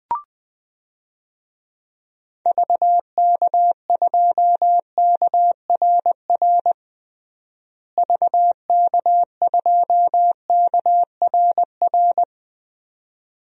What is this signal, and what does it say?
Morse code, heard as a keyed tone, about 20 wpm, 700 Hz. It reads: VK2KRR VK2KRR
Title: DXCC-100 Call Signs - 6 Characters - Repeated 2x - 20wpm